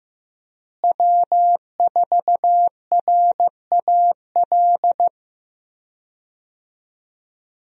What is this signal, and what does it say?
Morse code, heard as a keyed tone, about 15 wpm, 700 Hz. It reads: W4RAL
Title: USA Call Signs - 5 Characters - Repeated 2x - 15wpm